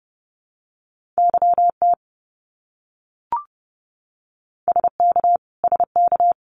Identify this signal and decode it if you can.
Morse code, heard as a keyed tone, about 30 wpm, 700 Hz. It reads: YT SK SK
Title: Canadian Provinces and Territories - Repeated 3x - 30wpm